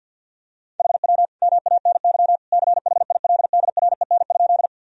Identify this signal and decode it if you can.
Morse code, heard as a keyed tone, about 52 wpm, 700 Hz. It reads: HW MANY CHILDREN?
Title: Sentences from Top 300 Words and Abbreviations 50wpm